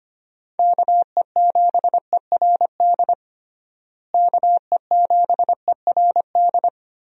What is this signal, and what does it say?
Morse code, heard as a keyed tone, about 25 wpm, 700 Hz. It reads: KE7ERD KE7ERD
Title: USA Call Signs - 6 Characters - Repeated 2x - 25wpm